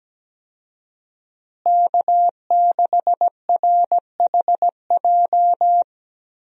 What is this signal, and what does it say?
Morse code, heard as a keyed tone, about 17 wpm, 700 Hz. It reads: K6RHJ
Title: USA Call Signs - 5 Characters - Repeated 2x - 17wpm